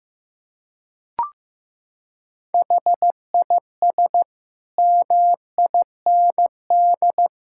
Morse code, heard as a keyed tone, about 15 wpm, 700 Hz. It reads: HIS MIND
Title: Sentences from Top 1000 Words - Review Part 6 - 15wpm